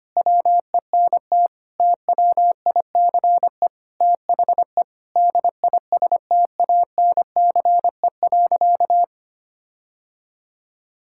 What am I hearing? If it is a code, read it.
Morse code, heard as a keyed tone, about 25 wpm, 700 Hz. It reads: WENT TWICE THE DISTANCE.
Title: Sentences from Top 1600 Words 25wpm